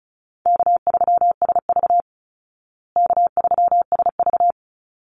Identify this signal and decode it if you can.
Morse code, heard as a keyed tone, about 34 wpm, 700 Hz. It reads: K3SV K3SV
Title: Easy Call Signs Repeated 2x - 35wpm